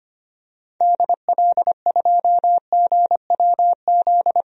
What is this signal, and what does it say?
Morse code, heard as a keyed tone, about 25 wpm, 700 Hz. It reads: DL2GWZ
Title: DXCC-100 Call Signs - 6 Characters - 25wpm